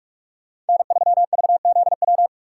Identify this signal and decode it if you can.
Morse code, heard as a keyed tone, about 46 wpm, 700 Hz. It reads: N3VZW
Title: USA Call Signs - 5 Characters - 4x Character Spacing - 45wpm